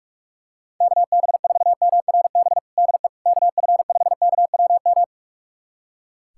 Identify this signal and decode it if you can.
Morse code, heard as a keyed tone, about 41 wpm, 700 Hz. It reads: KB4MRB DE KF5KWM
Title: USA Callsign DE Callsign 45wpm